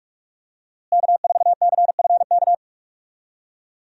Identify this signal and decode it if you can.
Morse code, heard as a keyed tone, about 41 wpm, 700 Hz. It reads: K4CFK
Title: USA Call Signs - 5 Characters - Repeated 3x - 45wpm